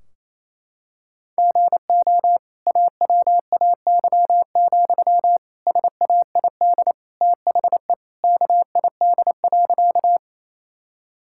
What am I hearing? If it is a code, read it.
Morse code, heard as a keyed tone, about 28 wpm, 700 Hz. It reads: GO AWAY, SAID THE KID.
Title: Sentences from Top 1600 Words 28wpm